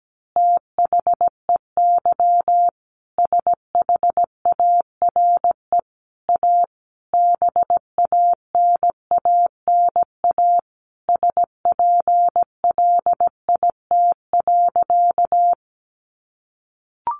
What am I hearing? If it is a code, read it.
Morse code, heard as a keyed tone, about 17 wpm, 700 Hz. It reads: THEY SHARE A BANANA SPLIT.
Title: Easy Phrases 17wpm